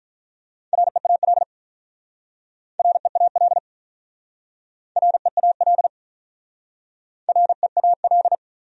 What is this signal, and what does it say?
Morse code, heard as a keyed tone, about 44 wpm, 700 Hz. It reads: REAL REAL REAL REAL
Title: Speed-Racing Top 500 Words 35wpm